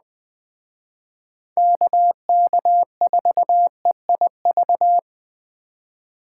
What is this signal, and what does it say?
Morse code, heard as a keyed tone, about 21 wpm, 700 Hz. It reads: KK4EIV
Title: Speed Racing - USA 2x3 Call Signs 17wpm